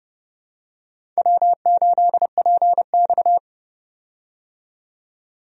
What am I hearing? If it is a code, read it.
Morse code, heard as a keyed tone, about 30 wpm, 700 Hz. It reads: W8PX
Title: USA Call Signs - 4 Characters - 30wpm